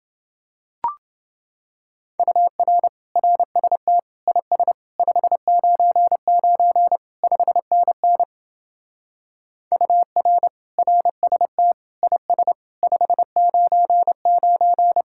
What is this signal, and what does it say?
Morse code, heard as a keyed tone, about 29 wpm, 700 Hz. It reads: UR RST IS 599 5NN UR RST IS 599